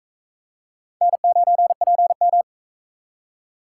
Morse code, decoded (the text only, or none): N9PM